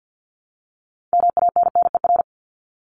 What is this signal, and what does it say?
Morse code, heard as a keyed tone, about 50 wpm, 700 Hz. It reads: MANNER
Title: Top 1501-1750 Words 50wpm